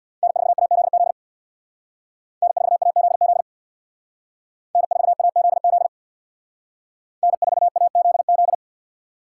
Morse code, heard as a keyed tone, about 62 wpm, 700 Hz. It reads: N4AZZ N4AZZ N4AZZ N4AZZ
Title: Speed Racing - USA 1x3 Call Signs 50wpm